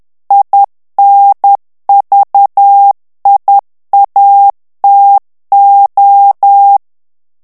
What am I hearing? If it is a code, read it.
Morse code, heard as a keyed tone, about 10 wpm, 800 Hz. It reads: INVIATO
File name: morse mp3 / ITALIEN texte clair 1 vitesse 10 mots minute